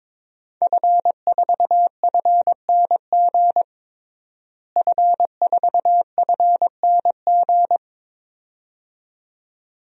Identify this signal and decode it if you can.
Morse code, heard as a keyed tone, about 21 wpm, 700 Hz. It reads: F4FNG F4FNG
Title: DXCC-100 Call Signs - 5 Characters - Repeated 2x - 22wpm